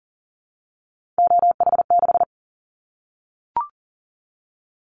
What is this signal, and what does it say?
Morse code, heard as a keyed tone, about 38 wpm, 700 Hz. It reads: OH6
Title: Callsign Prefixes for Top 25 DXCC - 40wpm